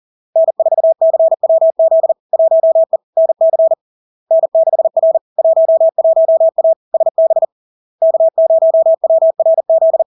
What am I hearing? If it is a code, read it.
Morse code, heard as a keyed tone, about 40 wpm, 650 Hz. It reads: N4CWZ 1E NC N6R 11A SB K0WRZ
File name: LARK-FD-2022-Log-in-CW-40wpm-650-Hz